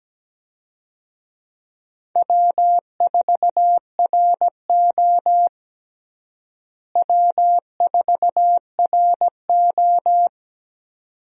Morse code, heard as a keyed tone, about 17 wpm, 700 Hz. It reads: W4RO W4RO
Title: USA Call Signs - 4 Characters - Repeated 2x - 17wpm